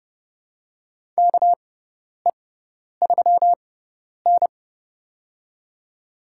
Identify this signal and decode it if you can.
Morse code, heard as a keyed tone, about 30 wpm, spaced as for 10 wpm, 700 Hz. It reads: KE3N
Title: USA Call Signs - 6x Character Spacing - 30wpm